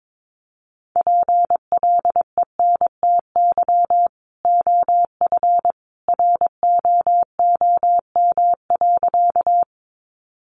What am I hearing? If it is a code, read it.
Morse code, heard as a keyed tone, about 22 wpm, 700 Hz. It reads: PLENTY OF ROOM.
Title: Sentences from Top 1300 Words 22wpm